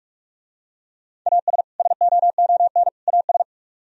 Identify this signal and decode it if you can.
Morse code, heard as a keyed tone, about 46 wpm, 700 Hz. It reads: AS SOON AS